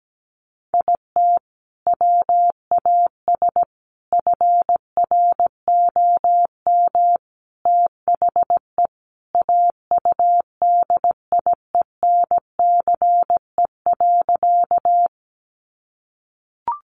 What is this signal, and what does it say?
Morse code, heard as a keyed tone, about 17 wpm, 700 Hz. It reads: IT WAS FROM THE AUDIENCE.
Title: Sentences from Top 800 Words 17wpm